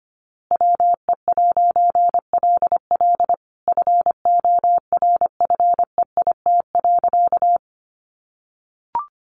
Morse code, heard as a keyed tone, about 25 wpm, 700 Hz. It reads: WE'LL FORFEIT.
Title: A Sound Of Thunder 25wpm